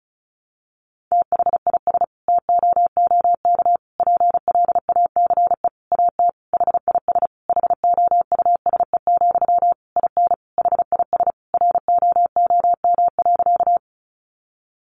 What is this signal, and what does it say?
Morse code, heard as a keyed tone, about 34 wpm, 700 Hz. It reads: THIS TOOK PLACE AT HIS HOUSE, IN HIS ROOM.
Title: Sentences from Top 300 Words 35wpm